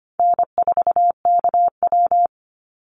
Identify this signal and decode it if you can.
Morse code, heard as a keyed tone, about 25 wpm, 700 Hz. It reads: N4KW
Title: Easy Call Signs Repeated 3x - 25wpm